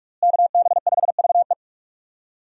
Morse code, heard as a keyed tone, about 44 wpm, 700 Hz. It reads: KB5VE
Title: USA Call Signs - 5 Characters - Repeated 3x - 45wpm